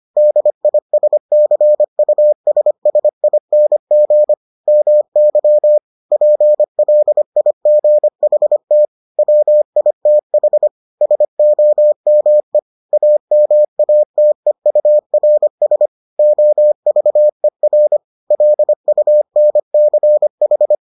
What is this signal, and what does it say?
Morse code, heard as a keyed tone, about 25 wpm, 600 Hz. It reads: DISCUSSING MY PLIGHT WITH SOME AMATEURS OVER LUNCH